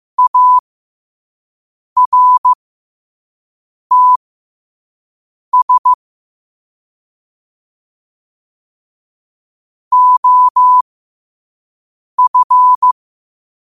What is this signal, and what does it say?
Morse code, heard as a keyed tone, about 15 wpm, spaced as for 5 wpm, 1000 Hz. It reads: ARTS OF
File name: qotdfiles / 5wpm